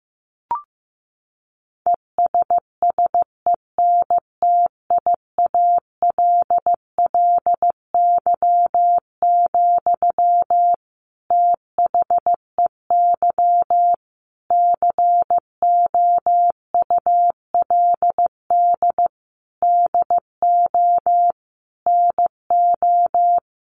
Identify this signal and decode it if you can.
Morse code, heard as a keyed tone, about 14 wpm, 700 Hz. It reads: ESSENTIALLY, THEY COULD DO NO
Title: Sentences from Top 1300 Words 15wpm